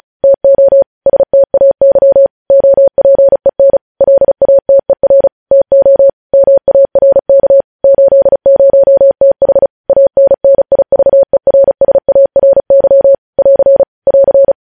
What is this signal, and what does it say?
Morse code, heard as a keyed tone, about 36 wpm, 550 Hz. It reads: TO STAY OPEN LATER TO MARK 80TH ANNIVERSARY <AR> <AR>
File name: podcast / News headlines-1x-35x35